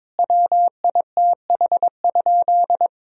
Morse code, heard as a keyed tone, about 22 wpm, 700 Hz. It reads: WITH?
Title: Sentences from Top 1000 Words - Review Part 1 - 22wpm